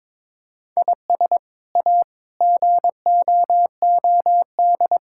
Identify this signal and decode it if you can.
Morse code, heard as a keyed tone, about 22 wpm, 700 Hz. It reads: IS A GOOD
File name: practice / 22wpm / Sets of 6 Words - 22wpm